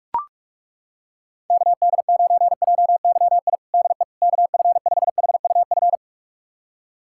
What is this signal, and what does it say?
Morse code, heard as a keyed tone, about 42 wpm, 700 Hz. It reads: KD9JYI DE KF5HUF
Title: USA Callsign DE Callsign 45wpm